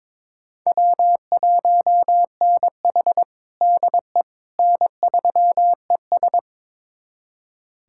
Morse code, heard as a keyed tone, about 22 wpm, 700 Hz. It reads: W1NH DE N3ES
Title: USA Callsign DE Callsign - 4 Characters - 22wpm